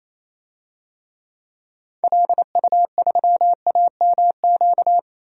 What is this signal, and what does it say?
Morse code, heard as a keyed tone, about 28 wpm, 700 Hz. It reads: LU3AMQ
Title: DXCC-100 Call Signs - 6 Characters - 28wpm